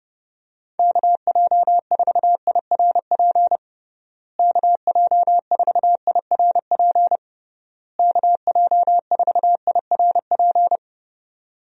Morse code, heard as a keyed tone, about 30 wpm, 700 Hz. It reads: KJ4IRP KJ4IRP KJ4IRP
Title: USA Call Signs - 6 Characters - Repeated 3x - 30wpm